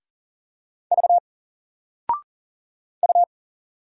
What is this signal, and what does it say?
Morse code, heard as a keyed tone, about 40 wpm, 700 Hz. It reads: V U
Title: Single Letter-Number 40wpm